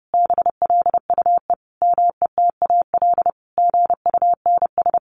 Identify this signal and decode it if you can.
Morse code, heard as a keyed tone, about 30 wpm, 700 Hz. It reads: BLUE METAL GUNS
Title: A Sound Of Thunder 30wpm